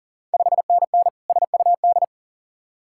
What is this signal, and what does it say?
Morse code, heard as a keyed tone, about 40 wpm, 700 Hz. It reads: 5NN SUD
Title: Ontario QSO Party Exchange 40wpm